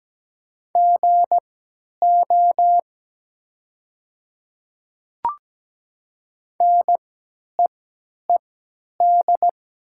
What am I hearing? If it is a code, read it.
Morse code, heard as a keyed tone, about 17 wpm, spaced as for 10 wpm, 690 Hz. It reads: GO NEED
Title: 3x Character Spacing - Top 200 Words - 17wpm